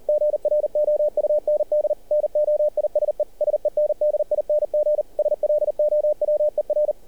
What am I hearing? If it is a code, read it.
Morse code, heard as a keyed tone, about 38 wpm, 600 Hz. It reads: GROUND NOISE SENDING SLOWER
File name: archived / CRSnet